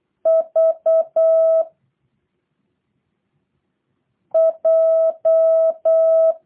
Morse code, heard as a keyed tone, about 8 wpm, 650 Hz. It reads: V J
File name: jh8cbh / 2023cw / review1-5